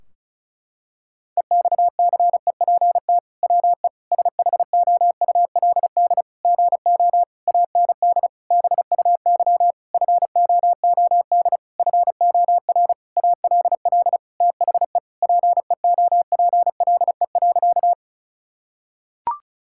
Morse code, heard as a keyed tone, about 34 wpm, 700 Hz. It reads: EXCEPT WE SHOULD GO AND BUY FOOD FOR ALL THE PEOPLE.
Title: Sentences from Top 1600 Words 35wpm